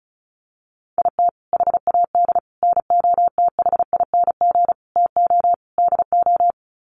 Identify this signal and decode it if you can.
Morse code, heard as a keyed tone, about 34 wpm, 700 Hz. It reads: IT HAD NOTHING TO DO
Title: Sets of 5 Words Encore2 35wpm